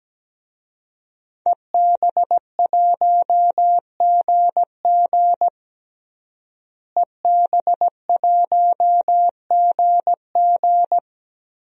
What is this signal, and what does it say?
Morse code, heard as a keyed tone, about 17 wpm, 700 Hz. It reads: EB1GG EB1GG
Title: DXCC-100 Call Signs - 5 Characters - Repeated 2x - 17wpm